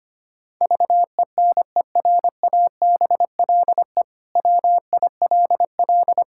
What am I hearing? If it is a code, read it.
Morse code, heard as a keyed tone, about 25 wpm, 700 Hz. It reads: VENERABLE WILL
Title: Pangram Challenge v2 - 25wpm